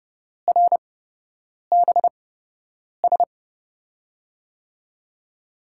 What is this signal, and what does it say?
Morse code, heard as a keyed tone, about 31 wpm, spaced as for 8 wpm, 700 Hz. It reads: RBS
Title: USA Call Signs - 5 Characters - 8x Character Spacing - 30wpm